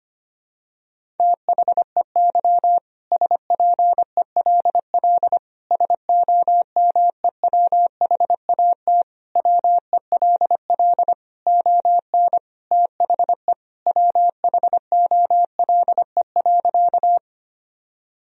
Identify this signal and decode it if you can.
Morse code, heard as a keyed tone, about 25 wpm, 700 Hz. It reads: THEY SPELL SOMEWHAT WELL ON THE WHOLE.
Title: Sentences from Top 1000 Words - Review Part 1 - 25wpm